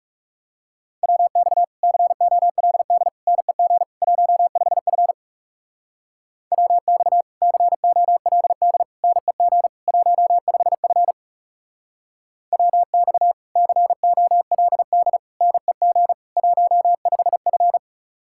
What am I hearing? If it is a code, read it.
Morse code, heard as a keyed tone, about 35 wpm, 700 Hz. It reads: WX COLD NEG 15F WX COLD NEG 15F WX COLD NEG 15F
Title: Speed-Racing Common QSO Phrases Encore - 30wpm